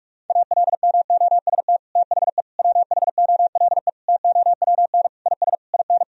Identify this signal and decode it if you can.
Morse code, heard as a keyed tone, about 42 wpm, 700 Hz. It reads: ALMOST THE WHOLE TOWN IS IN
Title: Sentences from Top 1000 Words - Review Part 4 - 45wpm